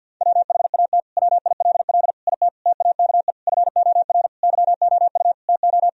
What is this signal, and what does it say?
Morse code, heard as a keyed tone, about 49 wpm, 700 Hz. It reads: WHAT WILL IT TAKE FOR YOU TO
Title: Sentences from Top 1000 Words - Review Part 3 - 50wpm